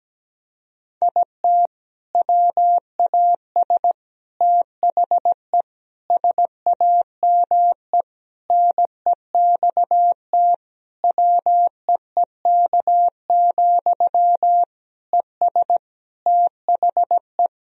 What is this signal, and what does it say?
Morse code, heard as a keyed tone, about 17 wpm, 700 Hz. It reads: IT WAS THE SAME NEXT WEEK, ES THE
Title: Sentences from Top 500 Words and Abbreviations 17wpm